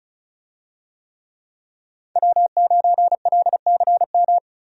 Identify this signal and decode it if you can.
Morse code, heard as a keyed tone, about 36 wpm, 700 Hz. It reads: W9LCM
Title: USA Call Signs - 5 Characters - 35wpm